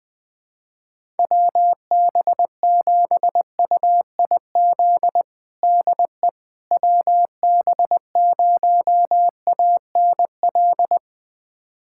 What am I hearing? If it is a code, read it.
Morse code, heard as a keyed tone, about 20 wpm, 700 Hz. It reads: WB7UIZ DE WB0ANL
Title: USA Callsign DE Callsign - 6 Characters - 20wpm